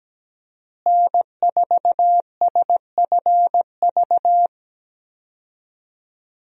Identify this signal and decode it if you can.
Morse code, heard as a keyed tone, about 17 wpm, 700 Hz. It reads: N4SFV
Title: USA Call Signs - 5 Characters - Repeated 2x - 17wpm